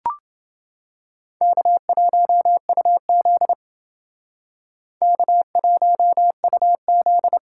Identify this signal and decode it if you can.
Morse code, heard as a keyed tone, about 28 wpm, 700 Hz. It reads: K1UZ K1UZ